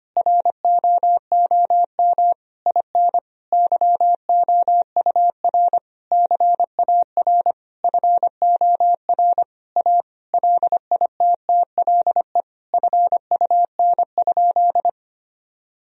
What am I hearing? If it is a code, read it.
Morse code, heard as a keyed tone, about 25 wpm, 700 Hz. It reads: ROOM IN YOUR CAR FOR A LITTLE FUN?
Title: Sentences from Top 1000 Words - Review Part 1 - 25wpm